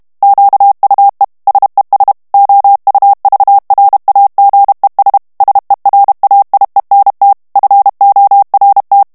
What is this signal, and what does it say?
Morse code, heard as a keyed tone, about 32 wpm, 800 Hz. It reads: QUE SES OUVRAGES SERAIENT FORT
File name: FRANCAIS TEXTE CLAIR 2 VITESSE 30 MOTS MINUTES